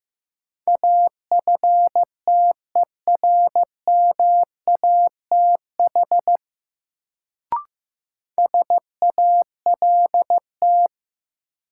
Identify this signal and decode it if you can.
Morse code, heard as a keyed tone, about 15 wpm, 700 Hz. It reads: AFTERMATH SALT
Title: Sets of 5 Words - New Word Drill - 15wpm